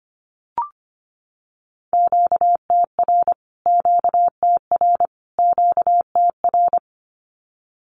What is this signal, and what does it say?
Morse code, heard as a keyed tone, about 25 wpm, 700 Hz. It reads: QTR QTR QTR
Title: Q-Codes Repeated 3x 25wpm